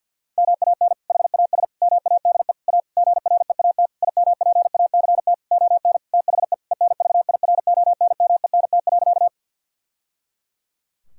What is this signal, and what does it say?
Morse code, heard as a keyed tone, about 50 wpm, 700 Hz. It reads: MAN HAS MADE A GREAT IMPACT ON THE ENVIRONMENT.
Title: Sentences from Top 1000 Words - Review Part 6 - 50wpm